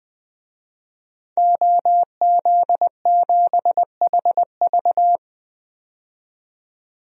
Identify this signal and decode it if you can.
Morse code, heard as a keyed tone, about 20 wpm, 700 Hz. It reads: OZ7HV
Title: DXCC-100 Call Signs - 5 Characters - 20wpm